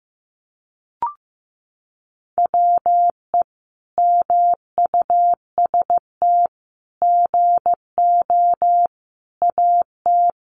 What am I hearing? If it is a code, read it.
Morse code, heard as a keyed tone, about 15 wpm, 700 Hz. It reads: WE MUST GO AT
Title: Sentences from Top 300 Words 15wpm